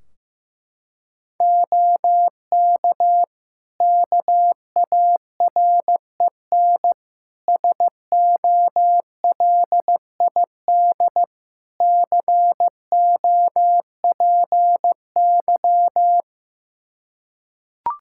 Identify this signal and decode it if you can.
Morse code, heard as a keyed tone, about 15 wpm, 700 Hz. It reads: OK KAREN SOLID COPY
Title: Practice QSO - Exchange 3 - More Info - 15wpm